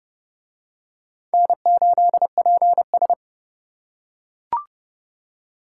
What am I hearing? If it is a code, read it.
Morse code, heard as a keyed tone, about 31 wpm, 700 Hz. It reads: N8PS